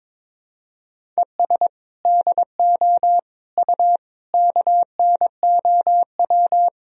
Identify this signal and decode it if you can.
Morse code, heard as a keyed tone, about 22 wpm, 700 Hz. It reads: ES DO U KNOW
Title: Sentences from Top 200 Words and Abbreviations 22wpm